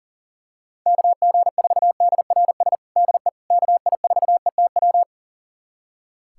Morse code, heard as a keyed tone, about 40 wpm, 700 Hz. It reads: KG4DRS DE KI4ETW
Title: USA Callsign DE Callsign - 6 Characters - 40wpm